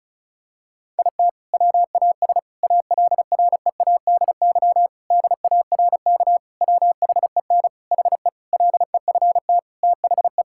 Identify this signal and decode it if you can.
Morse code, heard as a keyed tone, about 34 wpm, 700 Hz. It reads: IT WAS ALREADY DARK WHEN HE LEFT THE
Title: Sentences from Top 1000 Words - Review Part 5 - 35wpm